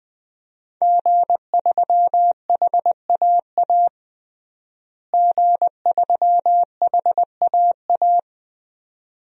Morse code, heard as a keyed tone, about 20 wpm, 700 Hz. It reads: G3HAA G3HAA
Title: DXCC-100 Call Signs - 5 Characters - Repeated 2x - 20wpm